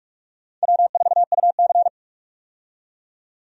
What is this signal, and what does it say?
Morse code, heard as a keyed tone, about 46 wpm, 700 Hz. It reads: W4UC
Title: USA Call Signs - 4 Characters - 45wpm